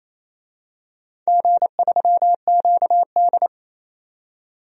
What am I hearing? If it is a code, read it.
Morse code, heard as a keyed tone, about 28 wpm, 690 Hz. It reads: G3QD
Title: DXCC-100 Call Signs - 4 Characters - 28wpm